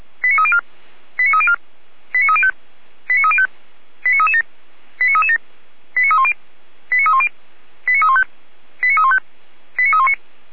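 Five-tone selective call (ZVEI), heard as quick runs of five tones, tones of about 70 ms. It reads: 89304 89304 89305 89305 89337 89337 89310 89310 89315 89315 89319
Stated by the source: ZVEI five-tone selective call 89229 received on an SDR.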